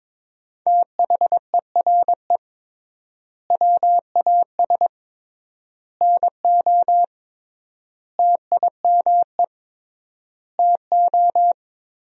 Morse code, heard as a keyed tone, about 22 wpm, 700 Hz. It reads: THERE WAS NO TIME TO
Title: 3x Word-Spacing - Sentences from Top 400 Words - 22wpm